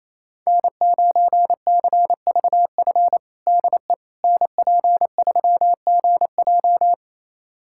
Morse code, heard as a keyed tone, about 28 wpm, 700 Hz. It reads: N9CVF DE NP3GJ
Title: USA Callsign DE Callsign - 5 Characters - 28wpm